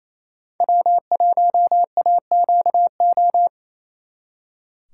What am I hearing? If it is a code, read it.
Morse code, heard as a keyed tone, about 35 wpm, 700 Hz. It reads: W1AQO